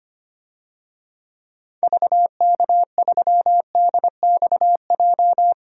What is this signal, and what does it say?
Morse code, heard as a keyed tone, about 25 wpm, 700 Hz. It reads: VK3DXJ
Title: DXCC-100 Call Signs - 6 Characters - 25wpm